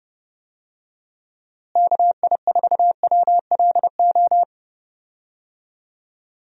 Morse code, heard as a keyed tone, about 30 wpm, 700 Hz. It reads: KI4WLO
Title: USA Call Signs - 6 Characters - Repeated 2x - 30wpm